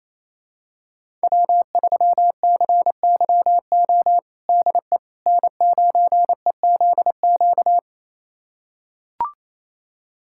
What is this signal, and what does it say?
Morse code, heard as a keyed tone, about 28 wpm, 690 Hz. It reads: W3CYO DE N9EZQ
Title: USA Callsign DE Callsign - 5 Characters - 28wpm